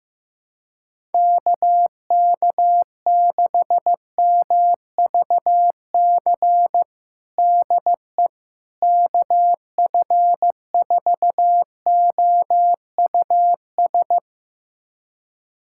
Morse code, heard as a keyed tone, about 15 wpm, 700 Hz. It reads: KK6MVC DE KF4OUS
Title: USA Callsign DE Callsign - 6 Characters - 15wpm